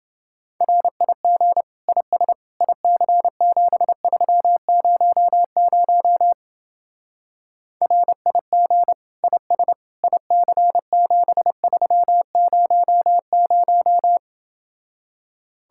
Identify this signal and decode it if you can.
Morse code, heard as a keyed tone, about 28 wpm, 700 Hz. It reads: RIG IS IC7300 RIG IS IC7300